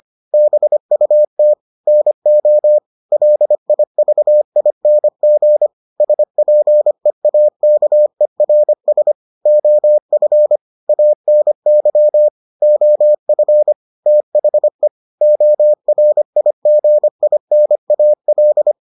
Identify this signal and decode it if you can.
Morse code, heard as a keyed tone, about 25 wpm, 600 Hz. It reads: BUT NO LIVING SPEAKERS OF ANY OF THE ORIGINAL